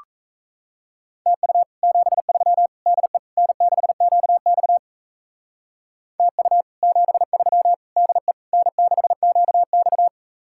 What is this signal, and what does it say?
Morse code, heard as a keyed tone, about 36 wpm, 700 Hz. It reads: TU 73 DE N6QX TU 73 DE N6QX